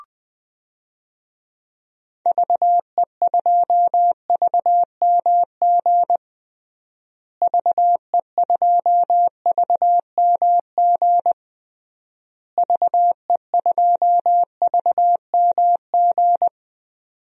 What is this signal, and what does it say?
Morse code, heard as a keyed tone, about 20 wpm, 700 Hz. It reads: VE2VMG VE2VMG VE2VMG
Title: DXCC-100 Call Signs - 6 Characters - Repeated 3x - 20wpm